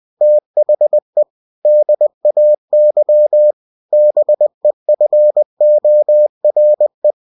Morse code, heard as a keyed tone, about 20 wpm, 600 Hz. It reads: THE DAY BEFORE
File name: cw-books / winnie-the-pooh / winnie-20wpm-page-0005